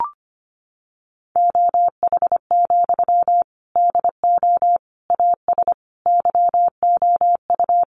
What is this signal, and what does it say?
Morse code, heard as a keyed tone, about 25 wpm, 700 Hz. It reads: OH, DO AS YOU
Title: Sentences from Top 500 Words 25wpm